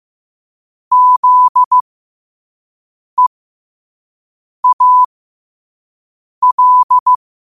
Morse code, heard as a keyed tone, about 15 wpm, spaced as for 5 wpm, 1000 Hz. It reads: ZEAL
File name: qotdfiles / 5wpm